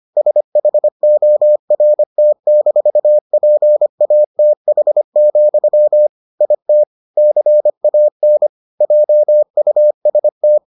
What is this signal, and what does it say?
Morse code, heard as a keyed tone, about 25 wpm, 600 Hz. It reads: SHORT-PATH, IT CAN JUST